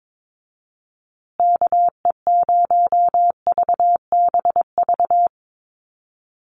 Morse code, heard as a keyed tone, about 22 wpm, 700 Hz. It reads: KE0VBV
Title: USA Call Signs Repeated 3x 22wpm